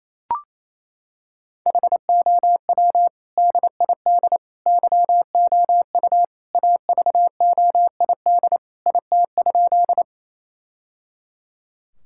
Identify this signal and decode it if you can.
Morse code, heard as a keyed tone, about 27 wpm, 700 Hz. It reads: HOW DID YOU AVOID IT?
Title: Sentences from Top 1000 Words - Review Part 6 - 28wpm